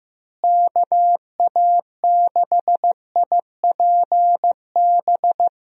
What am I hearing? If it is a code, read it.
Morse code, heard as a keyed tone, about 15 wpm, 700 Hz. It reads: KA6IPB
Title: USA Call Signs - 6 Characters - Repeated 2x - 15wpm